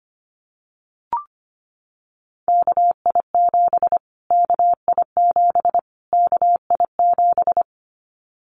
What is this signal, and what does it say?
Morse code, heard as a keyed tone, about 25 wpm, 700 Hz. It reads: KI7 KI7 KI7
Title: Callsign Prefixes for Top 25 DXCC - Repeated 3x - 25wpm